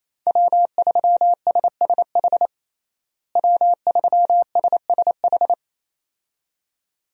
Morse code, heard as a keyed tone, about 28 wpm, 700 Hz. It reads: W3SSH W3SSH
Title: USA Call Signs - 5 Characters - Repeated 2x - 28wpm